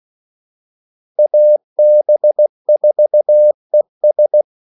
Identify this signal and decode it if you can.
Morse code, heard as a keyed tone, about 16 wpm, 600 Hz. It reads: AB4ES
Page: Kaldesignaler 16 wpm